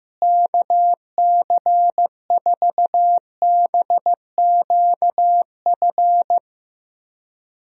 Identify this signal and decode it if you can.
Morse code, heard as a keyed tone, about 15 wpm, 700 Hz. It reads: KC4BQF
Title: USA Call Signs - 6 Characters - Repeated 3x - 15wpm